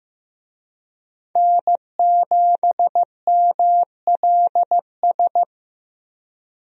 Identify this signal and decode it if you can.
Morse code, heard as a keyed tone, about 15 wpm, 700 Hz. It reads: N7MLS
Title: USA Call Signs - 5 Characters - Repeated 2x - 15wpm